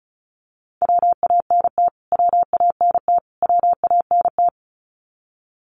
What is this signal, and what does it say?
Morse code, heard as a keyed tone, about 34 wpm, 700 Hz. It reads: WANT WANT WANT
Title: Top 300 Words Repeated 35wpm